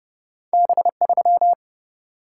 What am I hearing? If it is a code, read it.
Morse code, heard as a keyed tone, about 32 wpm, 700 Hz. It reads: B3
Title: Single-Letter Single-Number 30wpm